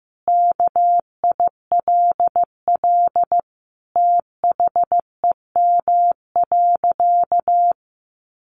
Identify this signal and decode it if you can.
Morse code, heard as a keyed tone, about 15 wpm, 700 Hz. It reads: KILL THEM.
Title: Sentences from Top 1400 Words 15wpm